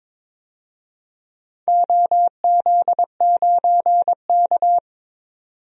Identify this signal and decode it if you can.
Morse code, heard as a keyed tone, about 22 wpm, 700 Hz. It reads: OZ9K
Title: DXCC-100 Call Signs - 4 Characters - 22wpm